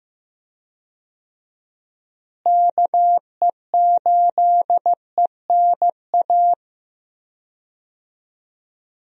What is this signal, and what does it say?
Morse code, heard as a keyed tone, about 15 wpm, 700 Hz. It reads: KE8ENA
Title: USA Call Signs - 6 Characters - Repeated 3x - 15wpm